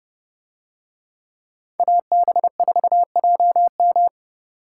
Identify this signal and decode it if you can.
Morse code, heard as a keyed tone, about 28 wpm, 700 Hz. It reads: AB4JM